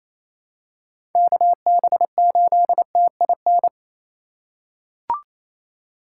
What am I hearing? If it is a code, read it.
Morse code, heard as a keyed tone, about 28 wpm, 700 Hz. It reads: KB8TIN
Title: USA Call Signs - 4x Character Spacing - 28wpm